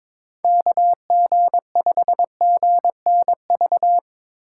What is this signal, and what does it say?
Morse code, heard as a keyed tone, about 22 wpm, 700 Hz. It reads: KG5GNV
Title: USA Call Signs - 6 Characters - 3x Character Spacing - 22wpm